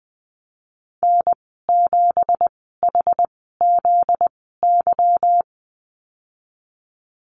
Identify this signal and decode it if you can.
Morse code, heard as a keyed tone, about 20 wpm, spaced as for 15 wpm, 700 Hz. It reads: N7HZY
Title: 245 Lesson 39g - Letter z - Call Signs 20wpm